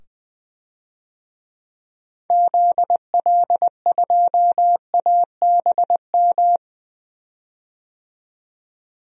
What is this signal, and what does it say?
Morse code, heard as a keyed tone, about 20 wpm, 700 Hz. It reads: ZL2ABM
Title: DXCC-100 Call Signs - 6 Characters - 20wpm